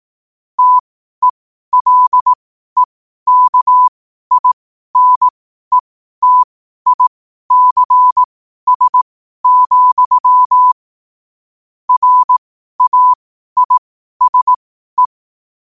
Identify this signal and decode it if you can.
Morse code, heard as a keyed tone, about 18 wpm, spaced as for 12 wpm, 1000 Hz. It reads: TELEKINETICS, RAISE
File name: qotdfiles / 12wpm